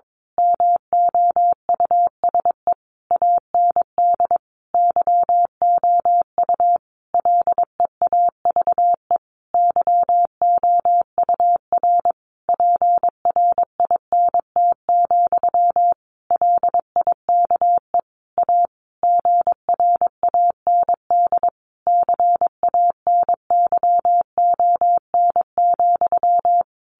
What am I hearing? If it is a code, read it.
Morse code, heard as a keyed tone, about 22 wpm, 700 Hz. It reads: MOUSE AND YOU LEAVE YOUR PRINT, LIKE A GRAND CANYON,
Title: A Sound Of Thunder 22wpm